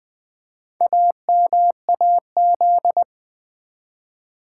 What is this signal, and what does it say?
Morse code, heard as a keyed tone, about 20 wpm, 700 Hz. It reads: AMAZ
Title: Build A Word - 2 Letters At A Time - 6-Letter Words - 20wpm